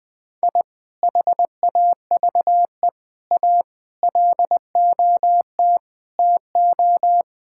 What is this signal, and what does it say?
Morse code, heard as a keyed tone, about 21 wpm, 700 Hz. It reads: I HAVE A LOT TO
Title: Sentences with 2-Word Phrases - 20wpm